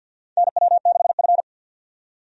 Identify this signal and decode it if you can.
Morse code, heard as a keyed tone, about 45 wpm, 700 Hz. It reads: NW6F
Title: USA Call Signs - 4 Characters - 50wpm